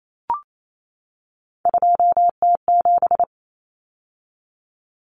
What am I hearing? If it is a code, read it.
Morse code, heard as a keyed tone, about 28 wpm, 700 Hz. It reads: 2T7
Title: Common Cut Numbers - 3-Digit 28wpm